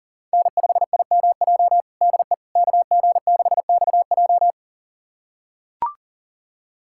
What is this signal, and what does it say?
Morse code, heard as a keyed tone, about 40 wpm, 700 Hz. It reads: N5IMJ DE KG6XJ
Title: USA Callsign DE Callsign - 5 Characters - 40wpm